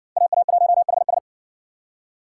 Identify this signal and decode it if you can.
Morse code, heard as a keyed tone, about 58 wpm, 700 Hz. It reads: AA1HS